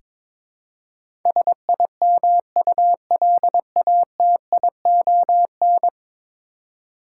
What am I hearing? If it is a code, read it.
Morse code, heard as a keyed tone, about 22 wpm, 700 Hz. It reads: SIMULATION
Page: morse-code-ninja-practice-22wpm